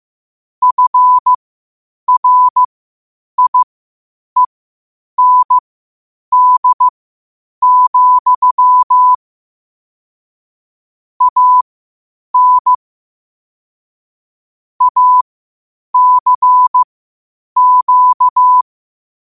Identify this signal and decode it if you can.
Morse code, heard as a keyed tone, about 15 wpm, spaced as for 8 wpm, 1000 Hz. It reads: FRIEND, AN ACQ
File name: qotdfiles / 8wpm